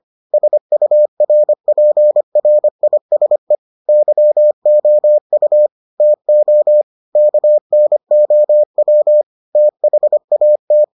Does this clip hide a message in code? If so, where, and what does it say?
Yes: Morse code from 0.3 to 10.8 s, heard as a keyed tone, about 25 wpm, 600 Hz. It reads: SURPRISE YOU TO KNOW THAT